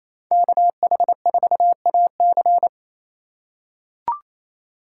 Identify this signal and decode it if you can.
Morse code, heard as a keyed tone, about 28 wpm, 700 Hz. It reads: KH4AC
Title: USA Call Signs - 2x Character Spacing - 28wpm